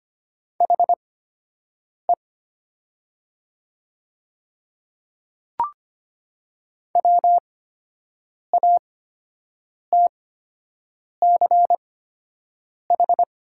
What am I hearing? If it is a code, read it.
Morse code, heard as a keyed tone, about 25 wpm, spaced as for 7 wpm, 700 Hz. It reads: HE WATCH
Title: 8x Character Spacing - Top 300 Words - 25wpm